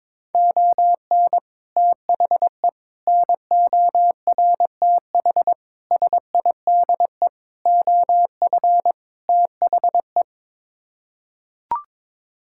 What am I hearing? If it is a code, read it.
Morse code, heard as a keyed tone, about 22 wpm, 700 Hz. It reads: ON THE NORTH SIDE OF THE
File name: practice / 22wpm / Sets of 6 Words - 22wpm